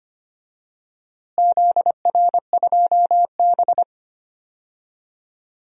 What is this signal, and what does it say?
Morse code, heard as a keyed tone, about 25 wpm, 700 Hz. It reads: ZR2B
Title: DXCC-100 Call Signs - 4 Characters - Repeated 3x - 25wpm